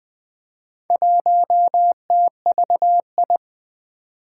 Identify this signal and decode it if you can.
Morse code, heard as a keyed tone, about 20 wpm, 700 Hz. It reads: 1TVI
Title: Four Challenging Letter-Number Combinations 20wpm